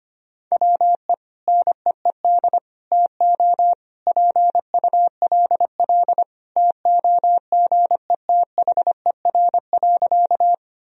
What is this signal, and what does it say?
Morse code, heard as a keyed tone, about 25 wpm, 700 Hz. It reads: WE NEED TO PULL TOGETHER.
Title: Sentences from Top 1000 Words - Review Part 2 - 25wpm